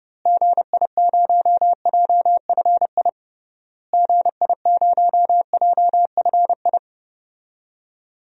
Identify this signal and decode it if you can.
Morse code, heard as a keyed tone, about 30 wpm, 700 Hz. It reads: GI0JFI GI0JFI
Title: DXCC-100 Call Signs - 6 Characters - Repeated 3x - 30wpm